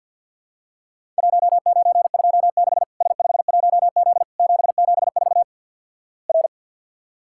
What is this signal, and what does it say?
Morse code, heard as a keyed tone, about 50 wpm, 700 Hz. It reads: 1926 S51Z 774 R
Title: Worked All Europe DX Contest - QTC Exchange with Repeats - 50wpm